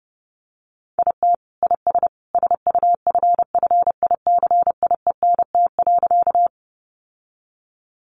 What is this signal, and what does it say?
Morse code, heard as a keyed tone, about 30 wpm, 700 Hz. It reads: IT IS SUFFICIENT.
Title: Sentences from Top 1400 Words 30wpm